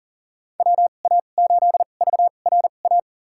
Morse code, heard as a keyed tone, about 39 wpm, spaced as for 28 wpm, 700 Hz. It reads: WA8VRA
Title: USA Call Signs - 6 Characters - 2x Character Spacing - 40wpm